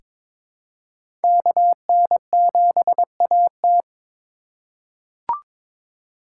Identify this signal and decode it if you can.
Morse code, heard as a keyed tone, about 23 wpm, 700 Hz. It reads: KN7AT
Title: USA Call Signs - 3x Character Spacing - 22wpm